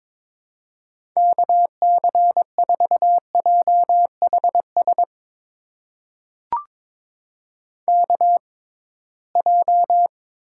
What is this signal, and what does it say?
Morse code, heard as a keyed tone, about 21 wpm, spaced as for 16 wpm, 700 Hz. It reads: KC4JHS KJ
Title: USA Call Signs - 6 Characters - 6x Character Spacing - 22wpm